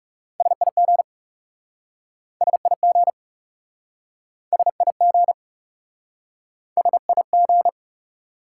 Morse code, heard as a keyed tone, about 40 wpm, 700 Hz. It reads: SIG SIG SIG SIG